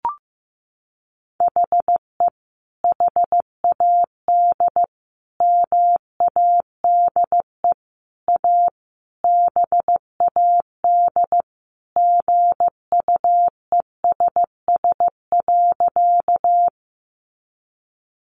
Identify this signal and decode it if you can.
Morse code, heard as a keyed tone, about 15 wpm, 700 Hz. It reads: HE HAD MADE A BAD GUESS.
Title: Sentences from Top 1500 Words 15wpm